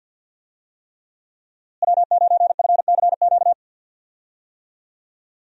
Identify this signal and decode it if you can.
Morse code, heard as a keyed tone, about 52 wpm, 700 Hz. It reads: W9FCQ
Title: USA Call Signs - 5 Characters - Repeated 3x - 50wpm